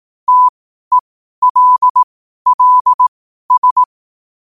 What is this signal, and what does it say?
Morse code, heard as a keyed tone, about 18 wpm, spaced as for 12 wpm, 1000 Hz. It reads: TELLS
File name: qotdfiles / 12wpm